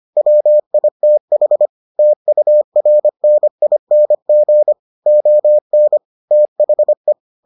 Morse code, heard as a keyed tone, about 25 wpm, 600 Hz. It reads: WITH TURNING ON THE